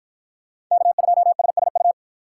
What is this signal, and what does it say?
Morse code, heard as a keyed tone, about 53 wpm, 700 Hz. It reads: K2SSU